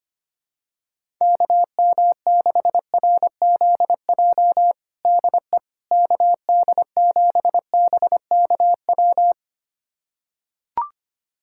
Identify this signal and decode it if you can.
Morse code, heard as a keyed tone, about 25 wpm, 700 Hz. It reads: KM6RZJ DE KD7BKW
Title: USA Callsign DE Callsign - 6 Characters - 25wpm